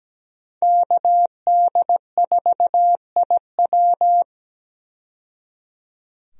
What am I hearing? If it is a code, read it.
Morse code, heard as a keyed tone, about 17 wpm, 700 Hz. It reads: KD4IW
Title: USA Call Signs - 5 Characters - 17wpm